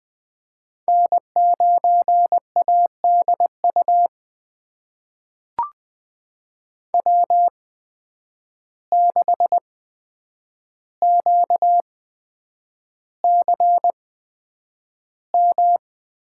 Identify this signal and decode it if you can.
Morse code, heard as a keyed tone, about 20 wpm, spaced as for 9 wpm, 700 Hz. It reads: N9ADU W6QCM
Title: USA Call Signs - 5 Characters - 8x Character Spacing - 20wpm